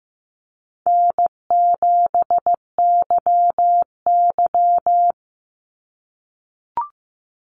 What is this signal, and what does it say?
Morse code, heard as a keyed tone, about 15 wpm, 700 Hz. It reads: N7YY
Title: NA SOTA Chaser Callsigns 15wpm